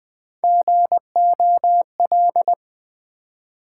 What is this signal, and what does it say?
Morse code, heard as a keyed tone, about 20 wpm, 700 Hz. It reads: GOL
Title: Build A Word - 1 Letter At A Time - 3 to 5-Letter Words - 20wpm